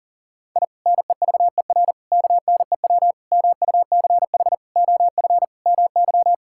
Morse code, heard as a keyed tone, about 40 wpm, 700 Hz. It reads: I NEVER KNEW MUCH OF MY
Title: Sentences from Top 1000 Words - Review Part 6 - 40wpm